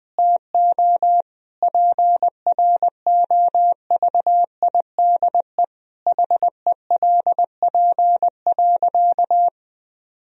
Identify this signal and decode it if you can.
Morse code, heard as a keyed tone, about 20 wpm, 700 Hz. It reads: TO PROVIDE HELP.
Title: Sentences from Top 1000 Words - Review Part 3 - 20wpm